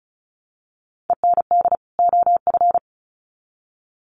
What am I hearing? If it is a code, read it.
Morse code, heard as a keyed tone, about 34 wpm, 700 Hz. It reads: END OF
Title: Sets of 2 Words 35wpm